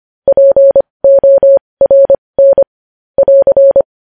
Morse code, heard as a keyed tone, about 25 wpm, spaced as for 20 wpm, 550 Hz. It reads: PORN <AR>
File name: podcast / News headlines-1x-25x20